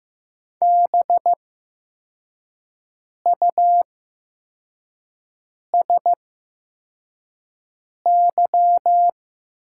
Morse code, heard as a keyed tone, about 15 wpm, spaced as for 4 wpm, 700 Hz. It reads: BUSY
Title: 8x Character Spacing - Top 500 Words - 15wpm